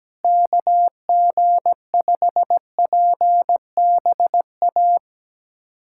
Morse code, heard as a keyed tone, about 17 wpm, 700 Hz. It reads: KG5PBA
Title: USA Callsign DE Callsign - 6 Characters - 17wpm